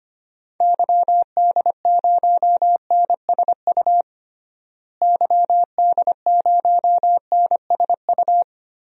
Morse code, heard as a keyed tone, about 25 wpm, 700 Hz. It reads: YD0NSU YD0NSU
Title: DXCC-100 Call Signs - 6 Characters - Repeated 3x - 25wpm